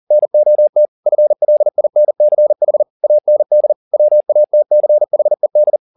Morse code, heard as a keyed tone, about 40 wpm, 600 Hz. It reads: NOT FLINCH AND WATCHED
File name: the old man and the sea 2 40wpm